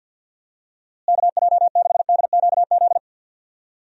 Morse code, heard as a keyed tone, about 47 wpm, 700 Hz. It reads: KJ6DQZ
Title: USA Call Signs - 6 Characters - Repeated 2x - 50wpm